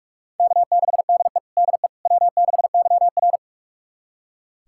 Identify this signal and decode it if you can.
Morse code, heard as a keyed tone, about 46 wpm, 700 Hz. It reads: K6DE DE W6YR
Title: USA Callsign DE Callsign - 4 Characters - 45wpm